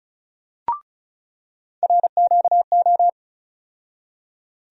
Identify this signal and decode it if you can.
Morse code, heard as a keyed tone, about 36 wpm, 700 Hz. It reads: RQO
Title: USA Call Sign Postfixes - 3 Letters 35wpm